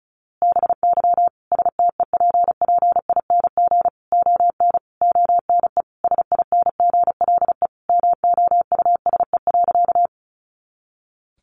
Morse code, heard as a keyed tone, about 34 wpm, 700 Hz. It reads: BY STEPPING ON ONE SINGLE MOUSE.
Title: A Sound Of Thunder 35wpm